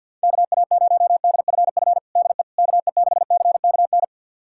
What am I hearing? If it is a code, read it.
Morse code, heard as a keyed tone, about 48 wpm, 700 Hz. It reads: KA0DFF DE KE6CKN
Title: USA Callsign DE Callsign 50wpm